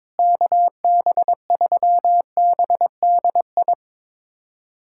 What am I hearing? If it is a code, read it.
Morse code, heard as a keyed tone, about 22 wpm, 700 Hz. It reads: KB3BDI
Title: USA Callsign DE Callsign 22wpm